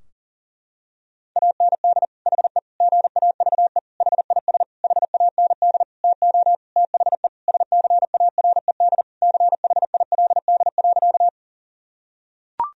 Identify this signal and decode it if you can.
Morse code, heard as a keyed tone, about 40 wpm, 700 Hz. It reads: AND HE GAVE HIS HAND TO THE SCARED CHILD.
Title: Sentences from Top 1000 Words - Review Part 4 - 40wpm